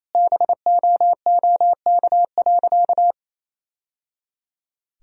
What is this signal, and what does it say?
Morse code, heard as a keyed tone, about 28 wpm, 700 Hz. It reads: BOOK.
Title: Sentences from Top 1000 Words - Review Part 3 - 28wpm